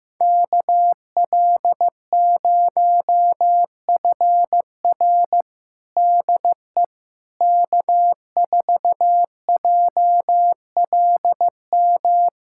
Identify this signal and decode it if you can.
Morse code, heard as a keyed tone, about 15 wpm, 700 Hz. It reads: KL0FR DE K4JLM
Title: USA Callsign DE Callsign - 5 Characters - 15wpm